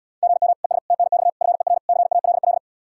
Morse code, heard as a keyed tone, about 78 wpm, 700 Hz. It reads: GM ES TNX FER CALL